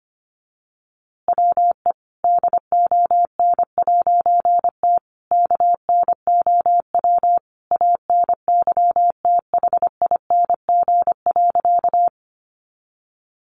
Morse code, heard as a keyed tone, about 25 wpm, 700 Hz. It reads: WE DON'T KNOW ANYTHING.
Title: A Sound Of Thunder 25wpm